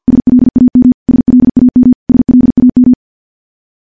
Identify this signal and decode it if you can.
Morse code, heard as a keyed tone, about 50 wpm, 250 Hz. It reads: SLAM SLAM SLAM
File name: morse / morse-training-50WPM